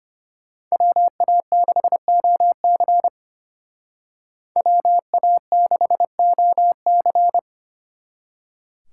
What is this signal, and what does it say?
Morse code, heard as a keyed tone, about 32 wpm, 700 Hz. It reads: WA6OC WA6OC